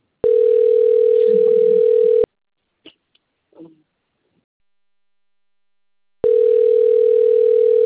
Ringback tone (North America), 440 + 480 Hz.